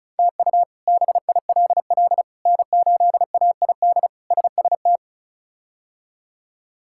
Morse code, heard as a keyed tone, about 34 wpm, 700 Hz. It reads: TU BILL N8AID SST
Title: K1USN SST QSO Practice - 35wpm